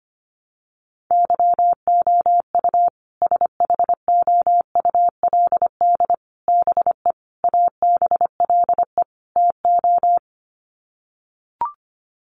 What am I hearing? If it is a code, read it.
Morse code, heard as a keyed tone, about 25 wpm, 700 Hz. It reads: YOU SHOULD BE ABLE TO
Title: Sets of 5 Words Encore 25wpm